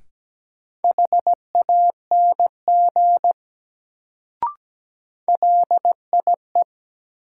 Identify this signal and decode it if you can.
Morse code, heard as a keyed tone, about 17 wpm, 700 Hz. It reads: HANG LIE
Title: Sentences from Top 1600 Words - New Word Drill Alt - 17wpm